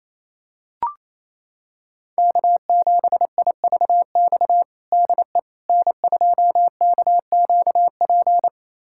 Morse code, heard as a keyed tone, about 28 wpm, 700 Hz. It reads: K7IVX DE N2KQP
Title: USA Callsign DE Callsign - 5 Characters - 28wpm